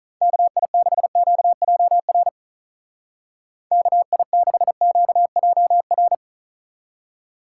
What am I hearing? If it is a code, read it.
Morse code, heard as a keyed tone, about 42 wpm, 700 Hz. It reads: KI6QJR KI6QJR
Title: Speed Racing - USA 2x3 Call Signs 35wpm